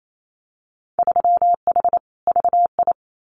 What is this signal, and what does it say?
Morse code, heard as a keyed tone, about 28 wpm, 700 Hz. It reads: 3H VI
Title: Winter Field Day Exchanges Encore 28wpm